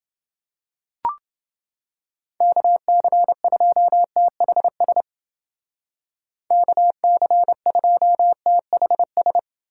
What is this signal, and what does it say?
Morse code, heard as a keyed tone, about 29 wpm, 700 Hz. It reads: KC2THS KC2THS